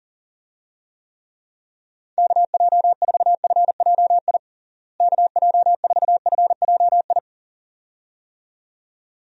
Morse code, heard as a keyed tone, about 40 wpm, 700 Hz. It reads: KJ4FJI KJ4FJI
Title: USA Call Signs - 6 Characters - Repeated 2x - 40wpm